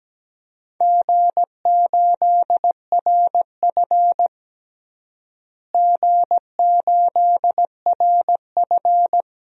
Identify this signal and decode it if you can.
Morse code, heard as a keyed tone, about 17 wpm, 700 Hz. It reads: G8RF G8RF
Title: DXCC-100 Call Signs - 4 Characters - Repeated 3x - 17wpm